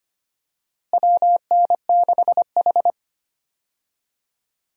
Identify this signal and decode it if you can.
Morse code, heard as a keyed tone, about 25 wpm, 700 Hz. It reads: WN6H
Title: USA Call Signs - 4 Characters - Repeated 2x - 25wpm